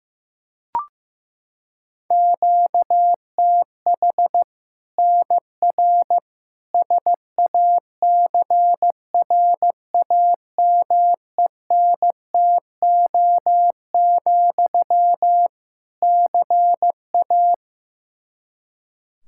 Morse code, heard as a keyed tone, about 15 wpm, 700 Hz. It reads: QTH NR SACRAMENTO, CA
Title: Practice QSO - Exchange 1 - Beginning - 15wpm